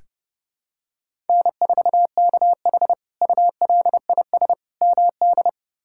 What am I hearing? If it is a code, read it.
Morse code, heard as a keyed tone, about 32 wpm, 700 Hz. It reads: N4KH ULIS MD
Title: K1USN SST QSO Practice - 30wpm